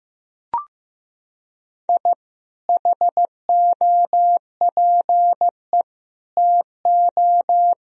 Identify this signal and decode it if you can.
Morse code, heard as a keyed tone, about 15 wpm, 700 Hz. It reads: I HOPE TO
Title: Sentences from Top 1000 Words - Review Part 1 - 15wpm